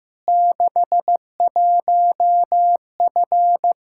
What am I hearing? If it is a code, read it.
Morse code, heard as a keyed tone, about 15 wpm, 700 Hz. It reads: 61F
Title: Practice QSO - Exchange 2 - Info - 15wpm